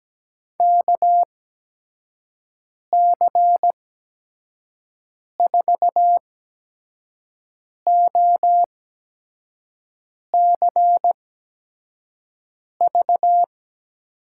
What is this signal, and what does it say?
Morse code, heard as a keyed tone, about 17 wpm, spaced as for 5 wpm, 700 Hz. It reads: KC4OCV
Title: USA Call Signs - 6 Characters - 8x Character Spacing - 17wpm